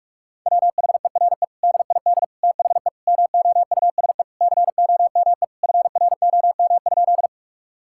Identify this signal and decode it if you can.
Morse code, heard as a keyed tone, about 41 wpm, 700 Hz. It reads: WHERE DID THE MOUSE COME FROM?
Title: Sentences from Top 1000 Words - Review Part 5 - 45wpm